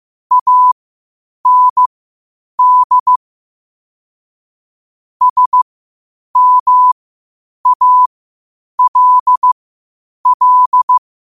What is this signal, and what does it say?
Morse code, heard as a keyed tone, about 15 wpm, spaced as for 8 wpm, 1000 Hz. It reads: AND SMALL
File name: qotdfiles / 8wpm